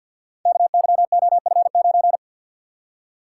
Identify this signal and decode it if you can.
Morse code, heard as a keyed tone, about 52 wpm, 700 Hz. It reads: KYOF9
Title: Code Groups - 5 Letter:Numbers - 50wpm